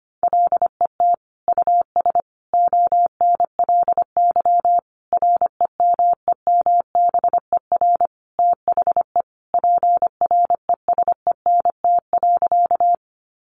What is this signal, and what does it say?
Morse code, heard as a keyed tone, about 25 wpm, 700 Hz. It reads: LET US ONLY REMEMBER THE PRESENT.
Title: Sentences from Top 800 Words 25wpm